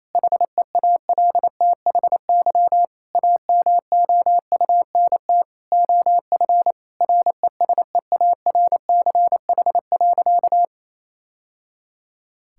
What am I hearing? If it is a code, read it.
Morse code, heard as a keyed tone, about 27 wpm, 700 Hz. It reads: HEALTHY AMOUNT OF RESEARCH.
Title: Sentences from Top 1000 Words - Review Part 5 - 28wpm